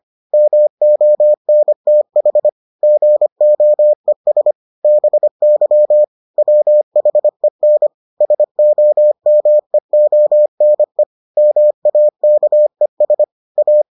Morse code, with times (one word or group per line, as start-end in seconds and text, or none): 0.3-2.5 MONTH
2.8-4.5 GOES
4.8-6.0 BY
6.4-7.9 WHEN
8.2-11.0 SOMEONE
11.4-13.2 MAKES
13.6-13.8 A